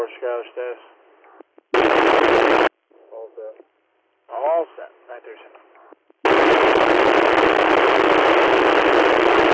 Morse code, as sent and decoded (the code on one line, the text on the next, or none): none